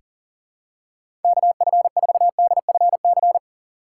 Morse code, heard as a keyed tone, about 40 wpm, 700 Hz. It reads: KF4DFC
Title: USA Call Signs - 6 Characters - 4x Character Spacing - 40wpm